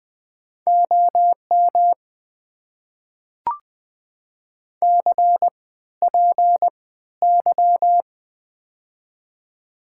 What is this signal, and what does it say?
Morse code, heard as a keyed tone, about 20 wpm, spaced as for 13 wpm, 690 Hz. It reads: OM CPY
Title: Top 90 QSO Elements - 3x Character Spacing - 20wpm